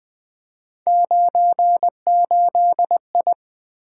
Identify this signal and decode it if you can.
Morse code, heard as a keyed tone, about 20 wpm, 700 Hz. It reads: 98I
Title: Three Letter-Number Combinations - 20wpm